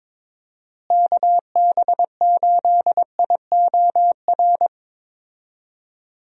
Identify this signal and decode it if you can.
Morse code, heard as a keyed tone, about 23 wpm, 700 Hz. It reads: KB8IOR
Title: USA Call Signs - 6 Characters - Repeated 3x - 22wpm